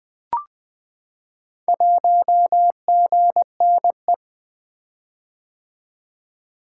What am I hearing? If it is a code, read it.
Morse code, heard as a keyed tone, about 20 wpm, 700 Hz. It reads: 1GNE
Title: Four Letter-Number Combinations - 20wpm